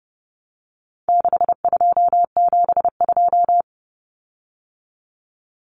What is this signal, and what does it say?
Morse code, heard as a keyed tone, about 30 wpm, 700 Hz. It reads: 6272
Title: Common Cut Numbers - 4-Digit 30wpm